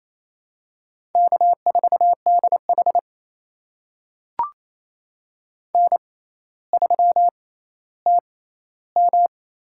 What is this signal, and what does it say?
Morse code, heard as a keyed tone, about 28 wpm, spaced as for 14 wpm, 700 Hz. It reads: K4DH N3TM
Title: USA Call Signs - 4 Characters - 6x Character Spacing - 28wpm